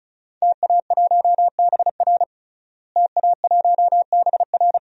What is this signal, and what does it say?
Morse code, heard as a keyed tone, about 36 wpm, 700 Hz. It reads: TA1BR TA1BR
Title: DXCC-100 Call Signs - 5 Characters - Repeated 2x - 35wpm